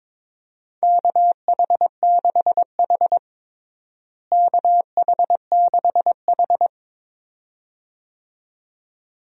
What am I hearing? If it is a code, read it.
Morse code, heard as a keyed tone, about 22 wpm, 700 Hz. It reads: KH6H KH6H
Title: DXCC-100 Call Signs - 4 Characters - Repeated 2x - 22wpm